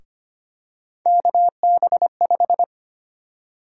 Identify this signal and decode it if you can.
Morse code, heard as a keyed tone, about 25 wpm, 700 Hz. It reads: KB5
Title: USA Call Sign Prefixes - Repeated 2x - 25wpm